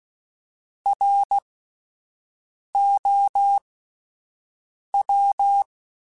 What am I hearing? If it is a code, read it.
Morse code, heard as a keyed tone, about 16 wpm, spaced as for 5 wpm, 800 Hz. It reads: ROW